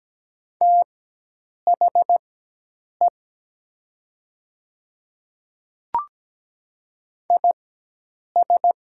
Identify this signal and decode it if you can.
Morse code, heard as a keyed tone, about 17 wpm, spaced as for 8 wpm, 700 Hz. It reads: THE IS
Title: 4x Character Spacing - Top 200 Words - 17wpm